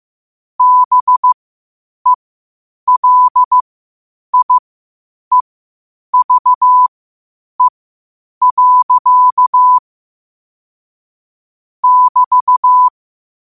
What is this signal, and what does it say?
Morse code, heard as a keyed tone, about 15 wpm, spaced as for 8 wpm, 1000 Hz. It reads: BELIEVE. <BT>
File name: qotdfiles / 8wpm